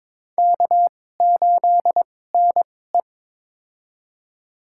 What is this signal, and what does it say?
Morse code, heard as a keyed tone, about 21 wpm, spaced as for 16 wpm, 700 Hz. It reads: K8NE
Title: USA Call Signs - 4 Characters - 2x Character Spacing - 22wpm